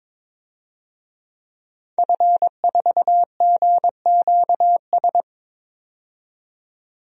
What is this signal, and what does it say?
Morse code, heard as a keyed tone, about 22 wpm, 700 Hz. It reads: F4GQS
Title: DXCC-100 Call Signs - 5 Characters - 22wpm